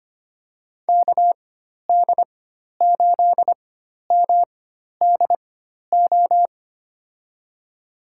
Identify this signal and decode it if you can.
Morse code, heard as a keyed tone, about 25 wpm, spaced as for 12 wpm, 700 Hz. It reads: KD8MDO
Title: USA Call Signs - 6 Characters - 4x Character Spacing - 25wpm